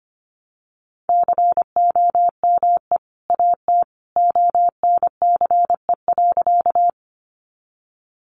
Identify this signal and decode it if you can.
Morse code, heard as a keyed tone, about 24 wpm, 700 Hz. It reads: COME AT ONCE.
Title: Sentences from Top 300 Words 25wpm